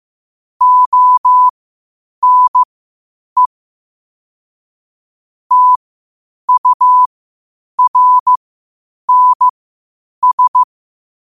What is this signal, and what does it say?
Morse code, heard as a keyed tone, about 15 wpm, spaced as for 8 wpm, 1000 Hz. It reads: ONE TURNS
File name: qotdfiles / 8wpm